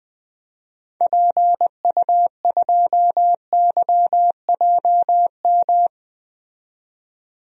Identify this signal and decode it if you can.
Morse code, heard as a keyed tone, about 21 wpm, 700 Hz. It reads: PU2YJM
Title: DXCC-100 Call Signs - 6 Characters - 20wpm